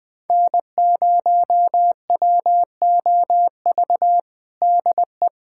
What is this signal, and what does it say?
Morse code, heard as a keyed tone, about 19 wpm, 700 Hz. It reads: N0WOV DE
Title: USA Callsign DE Callsign - 5 Characters - 20wpm